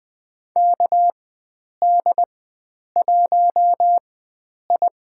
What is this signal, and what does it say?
Morse code, heard as a keyed tone, about 20 wpm, spaced as for 9 wpm, 700 Hz. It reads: KD1I
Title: USA Call Signs - 4x Character Spacing - 20wpm